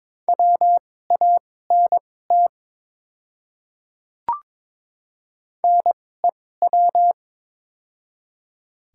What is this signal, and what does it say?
Morse code, heard as a keyed tone, about 23 wpm, spaced as for 16 wpm, 700 Hz. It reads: WANT NEW
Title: 2x Character Spacing - Top 200 Words - 22wpm